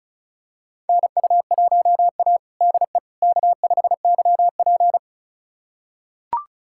Morse code, heard as a keyed tone, about 36 wpm, 700 Hz. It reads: NU1A DE K5YP
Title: USA Callsign DE Callsign - 4 Characters - 35wpm